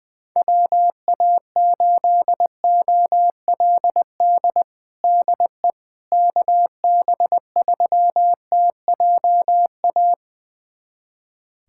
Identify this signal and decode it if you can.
Morse code, heard as a keyed tone, about 21 wpm, 700 Hz. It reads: WA8OLD DE KB3TJA
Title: USA Callsign DE Callsign - 6 Characters - 20wpm